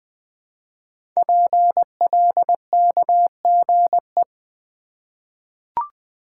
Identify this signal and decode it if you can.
Morse code, heard as a keyed tone, about 20 wpm, 700 Hz. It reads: PLKGE
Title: Code Groups - 5 Letters - 20wpm